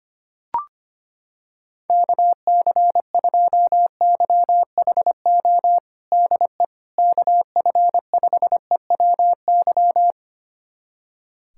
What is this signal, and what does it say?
Morse code, heard as a keyed tone, about 25 wpm, 700 Hz. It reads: KC2YHO DE KF5EWY
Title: USA Callsign DE Callsign - 6 Characters - 25wpm